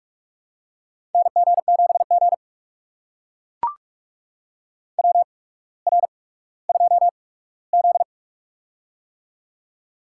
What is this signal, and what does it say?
Morse code, heard as a keyed tone, about 44 wpm, spaced as for 19 wpm, 700 Hz. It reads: NG7G WR2Z
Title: USA Call Signs - 4 Characters - 8x Character Spacing - 45wpm